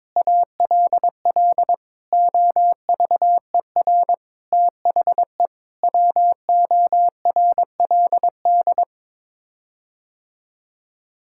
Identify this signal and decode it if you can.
Morse code, heard as a keyed tone, about 22 wpm, 700 Hz. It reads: ALL OVER THE WORLD
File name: practice / 22wpm / Sets of 6 Words - 22wpm